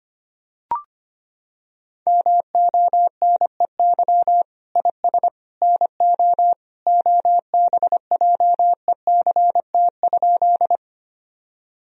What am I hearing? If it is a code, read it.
Morse code, heard as a keyed tone, about 25 wpm, 700 Hz. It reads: MONEY IS NO OBJECT?
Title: Sentences from Top 1000 Words - Review Part 3 - 25wpm